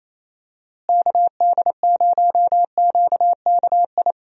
Morse code, heard as a keyed tone, about 28 wpm, 700 Hz. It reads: KD0QKI
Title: USA Call Signs - 6 Characters - Repeated 2x - 28wpm